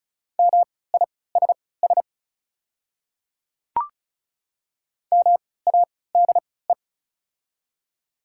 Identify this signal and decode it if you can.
Morse code, heard as a keyed tone, about 34 wpm, spaced as for 20 wpm, 700 Hz. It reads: MISS MADE
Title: 3x Character Spacing - Top 500 Words - 35wpm